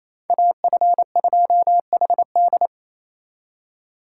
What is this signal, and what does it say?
Morse code, heard as a keyed tone, about 37 wpm, 700 Hz. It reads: AF2HD